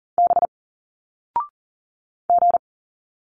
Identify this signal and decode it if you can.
Morse code, heard as a keyed tone, about 40 wpm, 700 Hz. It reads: B G
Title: Single Letters 40wpm